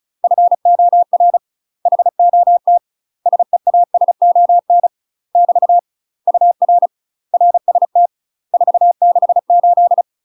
Morse code, heard as a keyed tone, about 34 wpm, 700 Hz. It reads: FOR HOT SEASON <BT> UR RST 468
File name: UK MASTER QSOs TRAINING/35WPM